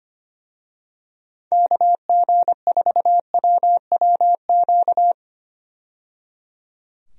Morse code, heard as a keyed tone, about 25 wpm, 700 Hz. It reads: KG4WWQ
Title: USA Call Signs - 6 Characters - 25wpm